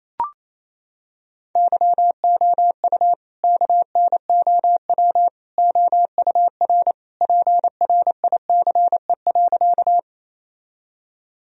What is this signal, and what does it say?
Morse code, heard as a keyed tone, about 28 wpm, 700 Hz. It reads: YOU KNOW OUR PRICE.
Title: Sentences from Top 1000 Words - Review Part 4 - 28wpm